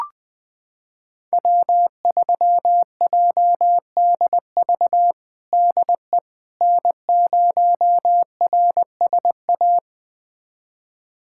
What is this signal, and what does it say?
Morse code, heard as a keyed tone, about 21 wpm, 700 Hz. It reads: W3JDV DE N0RSA
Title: USA Callsign DE Callsign - 5 Characters - 20wpm